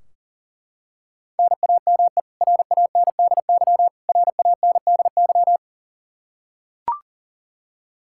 Morse code, heard as a keyed tone, about 40 wpm, 700 Hz. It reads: NAME RANDY RANDY
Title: Practice QSO - Exchange 1 - Beginning - 40wpm